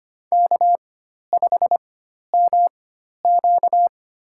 Morse code, heard as a keyed tone, about 25 wpm, spaced as for 12 wpm, 700 Hz. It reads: K5MQ
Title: USA Call Signs - 4 Characters - 4x Character Spacing - 25wpm